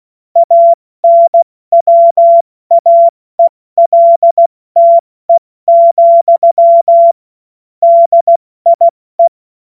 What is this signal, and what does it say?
Morse code, heard as a keyed tone, about 16 wpm, 680 Hz. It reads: ANWAELTE, DIE
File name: Morsekurs / Witze / Witz 20. September 2022 70 F80